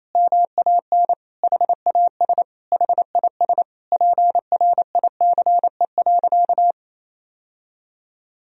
Morse code, heard as a keyed tone, about 28 wpm, 700 Hz. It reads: MAN HAS HIS PRICE.
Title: Sentences from Top 1000 Words - Review Part 4 - 28wpm